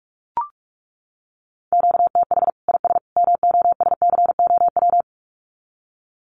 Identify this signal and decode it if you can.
Morse code, heard as a keyed tone, about 44 wpm, 700 Hz. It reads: QTH IS MOSCOW
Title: Common QSO Phrases 45wpm